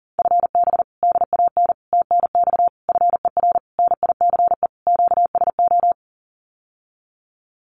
Morse code, heard as a keyed tone, about 40 wpm, 700 Hz. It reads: FB DAN TNX FER NICE QSO
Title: Common QSO Phrases Encore 40wpm